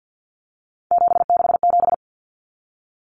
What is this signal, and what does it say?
Morse code, heard as a keyed tone, about 50 wpm, 700 Hz. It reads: 767
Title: Numbers - 3-Digit 50wpm